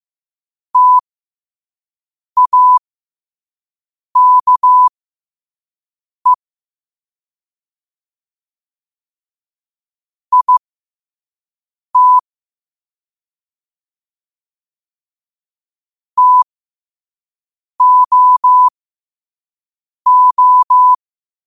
Morse code, heard as a keyed tone, about 15 wpm, spaced as for 5 wpm, 1000 Hz. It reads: TAKE IT TOO
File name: qotdfiles / 5wpm